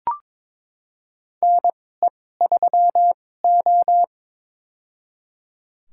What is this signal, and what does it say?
Morse code, heard as a keyed tone, about 22 wpm, spaced as for 16 wpm, 700 Hz. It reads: NE3O
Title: USA Call Signs - 4 Characters - 2x Character Spacing - 22wpm